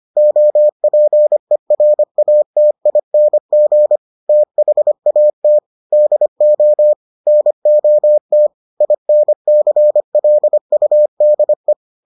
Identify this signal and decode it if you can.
Morse code, heard as a keyed tone, about 25 wpm, 600 Hz. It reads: OPERATING THAT DO NOT INCLUDE